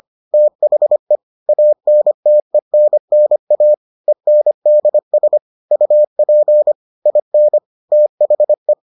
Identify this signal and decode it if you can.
Morse code, heard as a keyed tone, about 25 wpm, 600 Hz. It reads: THE ANTENNA ENDS UP IN THE